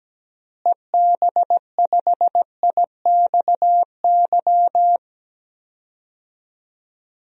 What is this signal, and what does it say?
Morse code, heard as a keyed tone, about 17 wpm, 700 Hz. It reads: EB5IXY
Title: DXCC-100 Call Signs - 6 Characters - Repeated 3x - 17wpm